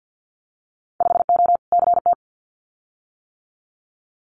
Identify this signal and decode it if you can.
Morse code, heard as a keyed tone, about 50 wpm, 700 Hz. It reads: 5O CT
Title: Winter Field Day Exchanges Encore 50wpm